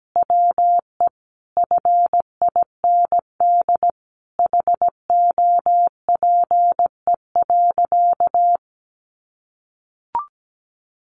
Morse code, heard as a keyed tone, about 17 wpm, 700 Hz. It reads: WE FIND HOPE.
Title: Sentences from Top 1300 Words 17wpm